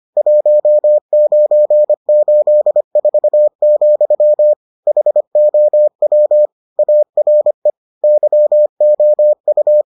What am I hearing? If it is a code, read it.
Morse code, heard as a keyed tone, about 25 wpm, 600 Hz. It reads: 1984, HOW ARE YOU